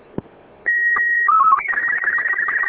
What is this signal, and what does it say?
SSTV (slow-scan television), Robot 36.